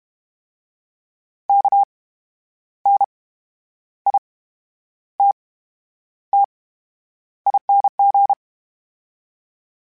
Morse code, heard as a keyed tone, about 32 wpm, spaced as for 10 wpm, 800 Hz. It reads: KNITTING
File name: Special INGs / Special INGs-mp3s / VerySlowToFaster Ing @30